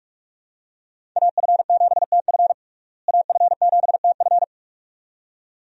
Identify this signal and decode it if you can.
Morse code, heard as a keyed tone, about 46 wpm, 700 Hz. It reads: AF7TF AF7TF
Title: USA Call Signs - 5 Characters - Repeated 2x - 45wpm